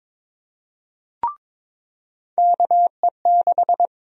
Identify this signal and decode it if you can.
Morse code, heard as a keyed tone, about 22 wpm, 700 Hz. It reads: KE6
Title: USA Call Sign Prefixes - 22wpm